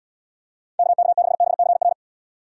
Morse code, heard as a keyed tone, about 75 wpm, 700 Hz. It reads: KK6BCR